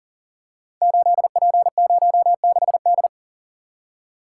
Morse code, heard as a keyed tone, about 39 wpm, 700 Hz. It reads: 8P06D
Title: Code Groups - 5 Letter:Numbers - 40wpm